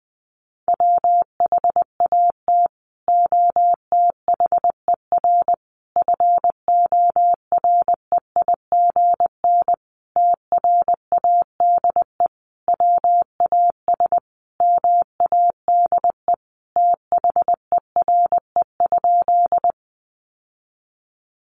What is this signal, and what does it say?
Morse code, heard as a keyed tone, about 20 wpm, 700 Hz. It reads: WHAT OTHER FOREIGN TRADE WAS MADE THERE?
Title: Sentences from Top 1400 Words 20wpm